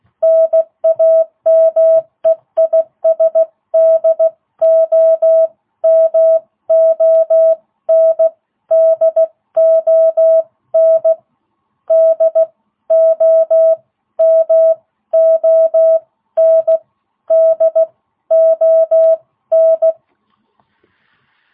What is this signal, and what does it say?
Morse code, heard as a keyed tone, about 16 wpm, spaced as for 13 wpm, 660 Hz. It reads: NAME IS DOMON DON DOMON DON